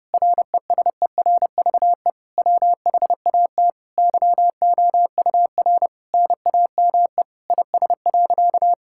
Morse code, heard as a keyed tone, about 30 wpm, 700 Hz. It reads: RESERVE WHAT YOUR NAME IS.
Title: Sentences from Top 1000 Words - Review Part 6 - 30wpm